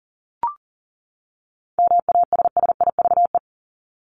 Morse code, heard as a keyed tone, about 40 wpm, 700 Hz. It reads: MASSIVE
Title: Top 1751-2000 Words 40wpm